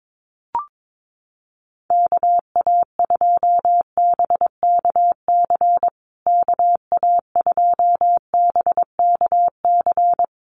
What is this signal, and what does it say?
Morse code, heard as a keyed tone, about 22 wpm, 690 Hz. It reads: KA2BKC KA2BKC
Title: USA Call Signs Repeated 3x 22wpm